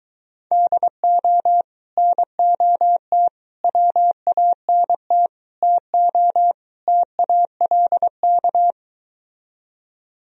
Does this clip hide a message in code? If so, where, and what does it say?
Yes: Morse code from 0.5 to 8.7 s, heard as a keyed tone, about 22 wpm, 700 Hz. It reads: DO NOT WANT TO TALK